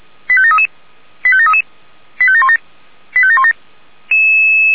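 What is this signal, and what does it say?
Five-tone selective call (ZVEI), heard as quick runs of five tones, tones of about 70 ms. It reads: 85733 85733 85717 85717
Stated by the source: Five-tone ZVEI selective call received by SDR.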